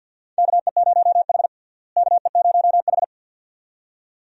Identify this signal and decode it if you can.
Morse code, heard as a keyed tone, about 47 wpm, 700 Hz. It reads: KE0H KE0H
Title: USA Call Signs - 4 Characters - Repeated 2x - 50wpm